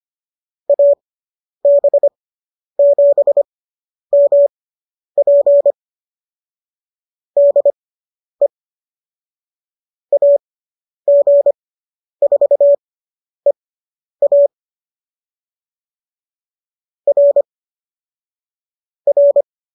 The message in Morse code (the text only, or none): AB7MP DE AG4EA R R